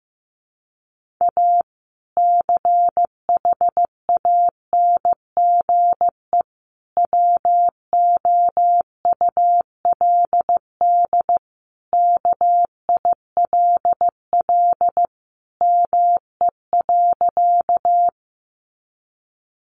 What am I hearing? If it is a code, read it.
Morse code, heard as a keyed tone, about 14 wpm, 700 Hz. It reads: A CHANGE WOULD KILL ME.
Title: Sentences from Top 1400 Words 15wpm